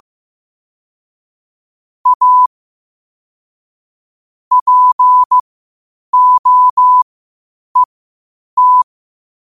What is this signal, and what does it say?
Morse code, heard as a keyed tone, about 15 wpm, spaced as for 8 wpm, 1000 Hz. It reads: A POET
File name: qotdfiles / 8wpm